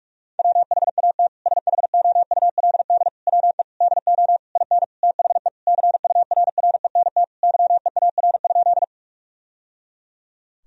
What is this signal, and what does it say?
Morse code, heard as a keyed tone, about 41 wpm, 700 Hz. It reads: WHAT SHOULD WE DO IN THE CURRENT YEAR?
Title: Sentences from Top 1000 Words - Review Part 4 - 45wpm